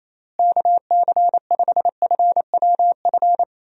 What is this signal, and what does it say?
Morse code, heard as a keyed tone, about 28 wpm, 700 Hz. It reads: KC5FWF
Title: USA Call Signs - 6 Characters - Repeated 2x - 28wpm